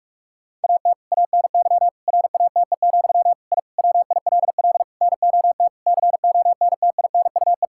Morse code, heard as a keyed tone, about 46 wpm, 700 Hz. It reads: AT ANY RATE, I WILL NOT CONTINUE
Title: Sentences from Top 1000 Words - Review Part 6 - 45wpm